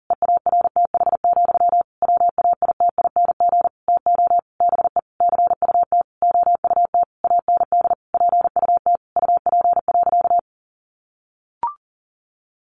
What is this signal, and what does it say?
Morse code, heard as a keyed tone, about 40 wpm, 700 Hz. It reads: EARTH, WAITING TO BE CUT OUT AND PUT UP.
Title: A Sound Of Thunder 40wpm